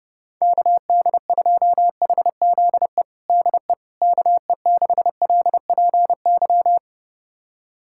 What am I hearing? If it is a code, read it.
Morse code, heard as a keyed tone, about 30 wpm, 700 Hz. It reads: KD2HZE DE KE6LPY
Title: USA Callsign DE Callsign - 6 Characters - 30wpm